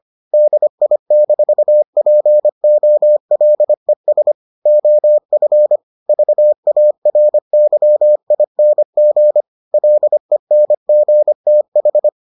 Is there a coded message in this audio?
Yes: Morse code from 0.3 to 12.1 s, heard as a keyed tone, about 25 wpm, 600 Hz. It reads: DI-POLES OF VARYING LENGTH